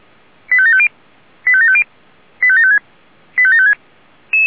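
Five-tone selective call (ZVEI), heard as quick runs of five tones, tones of about 70 ms. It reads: 85750 85750 85756 85756
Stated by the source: ZVEI five-tone selective call received on SDR.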